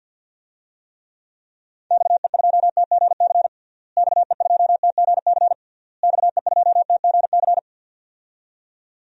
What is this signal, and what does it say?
Morse code, heard as a keyed tone, about 52 wpm, 700 Hz. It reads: XE2TGC XE2TGC XE2TGC
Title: DXCC-100 Call Signs - 6 Characters - Repeated 3x - 50wpm